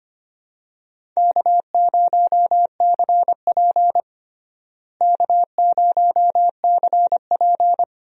Morse code, heard as a keyed tone, about 25 wpm, 700 Hz. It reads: K0CP K0CP
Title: USA Call Signs - 4 Characters - Repeated 2x - 25wpm